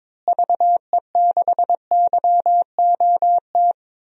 Morse code, heard as a keyed tone, about 22 wpm, 700 Hz. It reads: VE6YOT
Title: DXCC-100 Call Signs - 6 Characters - Repeated 3x - 22wpm